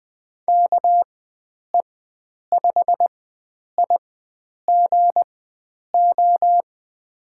Morse code, heard as a keyed tone, about 20 wpm, spaced as for 9 wpm, 700 Hz. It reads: KE5IGO
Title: USA Call Signs - 6 Characters - 4x Character Spacing - 20wpm